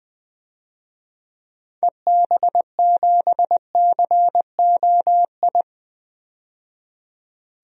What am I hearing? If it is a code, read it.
Morse code, heard as a keyed tone, about 20 wpm, 700 Hz. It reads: EB7COI
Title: DXCC-100 Call Signs - 6 Characters - 20wpm